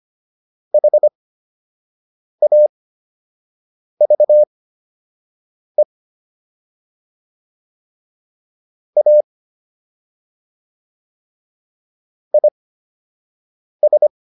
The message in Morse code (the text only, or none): HAVE A IS